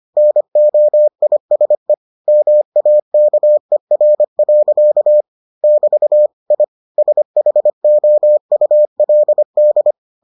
Morse code, heard as a keyed tone, about 25 wpm, 600 Hz. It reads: NOISE MAKER. <BT> I SHOULD